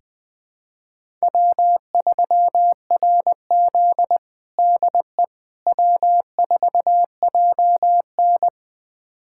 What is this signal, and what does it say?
Morse code, heard as a keyed tone, about 20 wpm, 700 Hz. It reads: W3RZ DE W4JN
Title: USA Callsign DE Callsign - 4 Characters - 20wpm